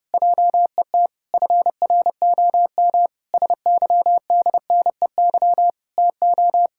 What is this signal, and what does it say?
Morse code, heard as a keyed tone, about 30 wpm, 700 Hz. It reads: JET FROM SYDNEY TO
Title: Pangram Challenge v2 - 30wpm